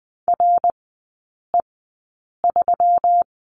Morse code, heard as a keyed tone, about 20 wpm, spaced as for 8 wpm, 700 Hz. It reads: RE3
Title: 198 Lesson 32e - Letter k - Three Characters 20wpm